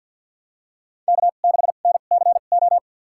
Morse code, heard as a keyed tone, about 52 wpm, spaced as for 36 wpm, 700 Hz. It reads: K6NCO
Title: USA Call Signs - 5 Characters - 2x Character Spacing - 50wpm